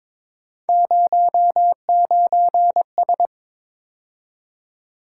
Morse code, heard as a keyed tone, about 22 wpm, 700 Hz. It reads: 09S
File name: practice / 22wpm / Three Letter-Number Combinations - 22wpm